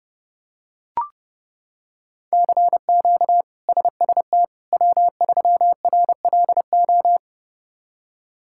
Morse code, heard as a keyed tone, about 30 wpm, 700 Hz. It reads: CQ SST W3RLO
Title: K1USN SST QSO Practice - 30wpm